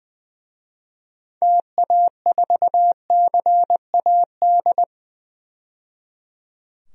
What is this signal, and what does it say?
Morse code, heard as a keyed tone, about 21 wpm, 700 Hz. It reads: TA4CAD
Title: DXCC-100 Call Signs - 6 Characters - 20wpm